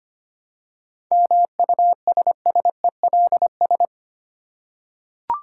Morse code, heard as a keyed tone, about 25 wpm, 700 Hz. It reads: MUSSELS
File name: practice / 25wpm / Difficult Words with Hints - 25wpm